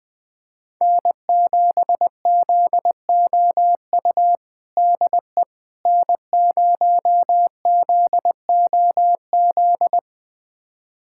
Morse code, heard as a keyed tone, about 21 wpm, 700 Hz. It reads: N7ZOU DE N0ZOZ
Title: USA Callsign DE Callsign - 5 Characters - 20wpm